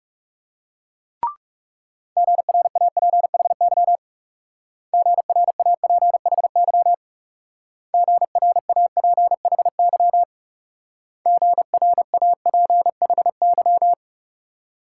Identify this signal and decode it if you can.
Morse code, heard as a keyed tone, about 36 wpm, 700 Hz. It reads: GRAPHY GRAPHY GRAPHY GRAPHY
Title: Speed-Racing Multi-letter Phonemes - 30wpm